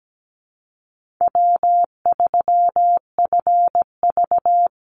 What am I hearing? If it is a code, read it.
Morse code, heard as a keyed tone, about 17 wpm, 700 Hz. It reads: W3FV
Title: Easy Call Signs - 17wpm